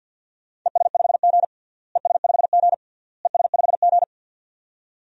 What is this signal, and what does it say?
Morse code, heard as a keyed tone, about 53 wpm, 700 Hz. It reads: ES5G ES5G ES5G
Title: DXCC-100 Call Signs - 4 Characters - Repeated 3x - 50wpm